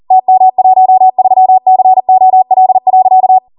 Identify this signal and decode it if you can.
Morse code, heard as a keyed tone, about 40 wpm, 750 Hz. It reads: TM13COL.